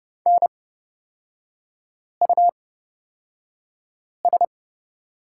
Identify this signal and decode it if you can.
Morse code, heard as a keyed tone, about 30 wpm, spaced as for 5 wpm, 700 Hz. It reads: NUS
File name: Lesson 36e - BT - Three Characters 30wpm